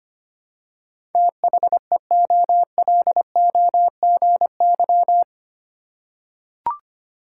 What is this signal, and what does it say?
Morse code, heard as a keyed tone, about 25 wpm, 700 Hz. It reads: THEOLOGY
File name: 8 Letter Words - First 3 Letters Given - 25wpm